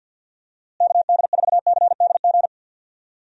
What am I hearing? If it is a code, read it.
Morse code, heard as a keyed tone, about 50 wpm, 700 Hz. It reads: KD4CDG
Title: USA Call Signs - 6 Characters - Repeated 2x - 50wpm